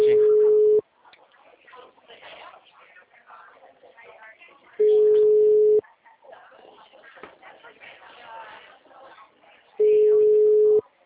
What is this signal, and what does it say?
Ringback tone (Europe, 425 Hz), with speech.